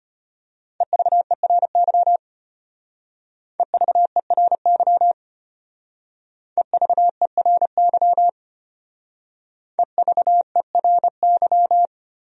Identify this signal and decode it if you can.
Morse code, heard as a keyed tone, about 31 wpm, 700 Hz. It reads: EVERY EVERY EVERY EVERY
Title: Speed-Racing Top 200 Words 25wpm